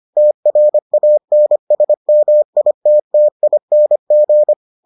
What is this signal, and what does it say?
Morse code, heard as a keyed tone, about 25 wpm, 600 Hz. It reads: TRANSMITTING